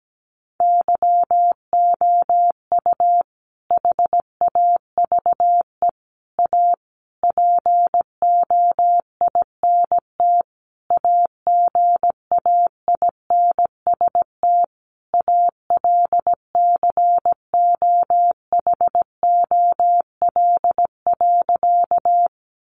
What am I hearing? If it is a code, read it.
Morse code, heard as a keyed tone, about 17 wpm, 700 Hz. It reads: YOU HAVE A POINT AGAINST ALCOHOL.
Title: Sentences from Top 1400 Words 17wpm